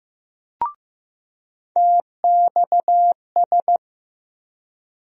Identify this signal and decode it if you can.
Morse code, heard as a keyed tone, about 15 wpm, 700 Hz. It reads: TXS
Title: USA Call Sign Postfixes - 3 Letters 15wpm